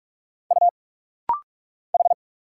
Morse code, heard as a keyed tone, about 46 wpm, 700 Hz. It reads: U H
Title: Single Letters - Rapid-Fire 45wpm